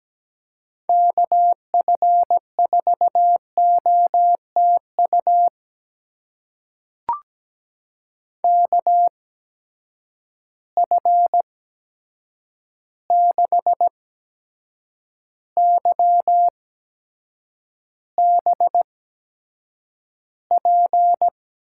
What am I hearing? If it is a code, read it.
Morse code, heard as a keyed tone, about 17 wpm, spaced as for 7 wpm, 700 Hz. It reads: KF4OTU KF6YBP
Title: USA Call Signs - 6 Characters - 8x Character Spacing - 17wpm